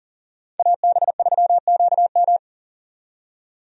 Morse code, heard as a keyed tone, about 40 wpm, 700 Hz. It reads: AB3QM
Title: USA Call Signs - 5 Characters - 40wpm